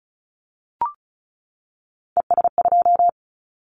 Morse code, heard as a keyed tone, about 34 wpm, 700 Hz. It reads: ES2
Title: Callsign Prefixes for Top 100 DXCC - 35wpm